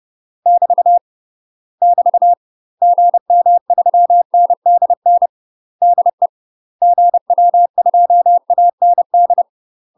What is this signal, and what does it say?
Morse code, heard as a keyed tone, about 30 wpm, 700 Hz. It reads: <BT> <BT> GM3NDN DE GW2AND
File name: UK MASTER QSOs TRAINING / 30WPM / UK MASTER QSOs 0001